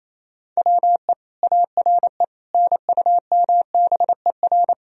Morse code, heard as a keyed tone, about 28 wpm, 700 Hz. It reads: WE ARE NUMBER
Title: Sentences from Top 1000 Words - Review Part 1 - 28wpm